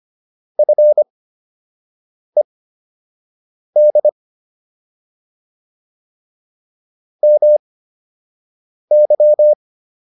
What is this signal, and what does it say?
Morse code, heard as a keyed tone, about 25 wpm, spaced as for 6 wpm, 600 Hz. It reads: FED MY